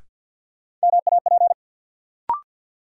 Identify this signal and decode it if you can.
Morse code, heard as a keyed tone, about 51 wpm, 700 Hz. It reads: MAP
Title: Sentences from Top 400 Words - New Word Drill - 50wpm